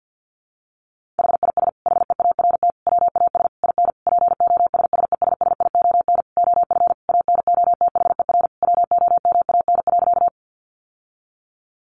Morse code, heard as a keyed tone, about 49 wpm, 700 Hz. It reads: HIS HEART WAS IN POSSESSION OF ANOTHER WOMAN.
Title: Sentences from Top 1400 Words 50wpm